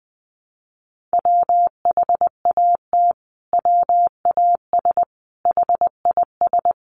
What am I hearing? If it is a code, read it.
Morse code, heard as a keyed tone, about 20 wpm, 700 Hz. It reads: WHAT WAS HIS
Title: Sentences from Top 1300 Words 20wpm